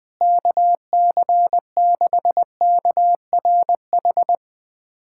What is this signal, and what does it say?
Morse code, heard as a keyed tone, about 20 wpm, 700 Hz. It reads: KC6KRH
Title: USA Callsign DE Callsign 20wpm